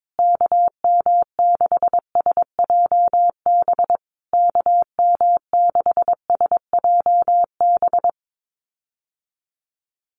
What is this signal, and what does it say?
Morse code, heard as a keyed tone, about 22 wpm, 700 Hz. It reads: KM6SJB KM6SJB
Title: USA Call Signs Repeated 3x 22wpm